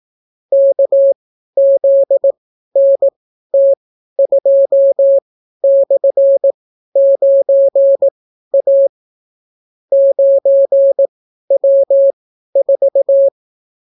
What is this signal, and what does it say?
Morse code, heard as a keyed tone, about 18 wpm, spaced as for 12 wpm, 550 Hz. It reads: KZNT2/9A 9W4